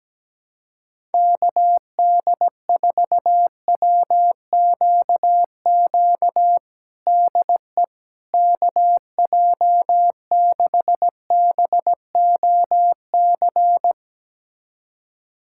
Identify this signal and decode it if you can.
Morse code, heard as a keyed tone, about 17 wpm, 700 Hz. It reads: KD4WQQ DE KJ6BOC
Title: USA Callsign DE Callsign - 6 Characters - 17wpm